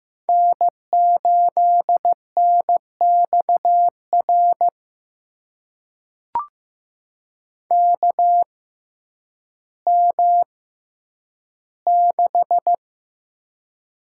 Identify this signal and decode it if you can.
Morse code, heard as a keyed tone, about 15 wpm, spaced as for 9 wpm, 700 Hz. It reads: N8NXR KM6
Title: USA Call Signs - 6x Character Spacing - 15wpm